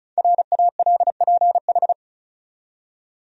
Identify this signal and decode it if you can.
Morse code, heard as a keyed tone, about 36 wpm, 700 Hz. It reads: RALPH
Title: USA Common Names - Max 5 Letters - 35wpm